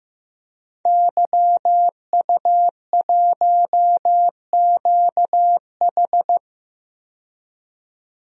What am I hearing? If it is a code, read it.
Morse code, heard as a keyed tone, about 15 wpm, 700 Hz. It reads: YU1QH
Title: DXCC-100 Call Signs - 5 Characters - 15wpm